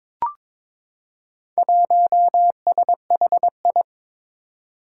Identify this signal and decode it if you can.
Morse code, heard as a keyed tone, about 22 wpm, 700 Hz. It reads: 1SHI
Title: Four Challenging Letter-Number Combinations 22wpm